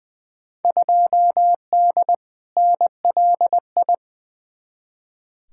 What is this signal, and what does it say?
Morse code, heard as a keyed tone, about 20 wpm, 700 Hz. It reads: 2D NLI
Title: Field Day Exchanges Encore - 20wpm